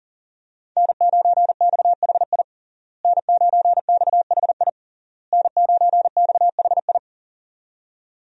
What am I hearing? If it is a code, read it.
Morse code, heard as a keyed tone, about 40 wpm, 700 Hz. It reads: N9XHI N9XHI N9XHI
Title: USA Call Signs - 5 Characters - Repeated 3x - 40wpm